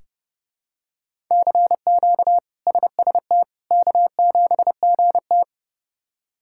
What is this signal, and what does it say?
Morse code, heard as a keyed tone, about 30 wpm, 700 Hz. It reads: CQ SST K7GT
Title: K1USN SST QSO Practice - 30wpm